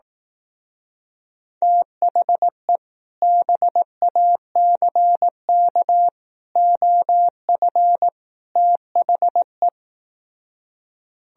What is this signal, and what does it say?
Morse code, heard as a keyed tone, about 18 wpm, 700 Hz. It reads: THE BACK OF THE
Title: Speed-Racing - Sets of 4 Words 15wpm